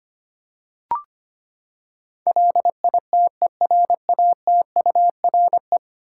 Morse code, heard as a keyed tone, about 25 wpm, 700 Hz. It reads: LITERATURE
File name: Ten Letter Words - 25wpm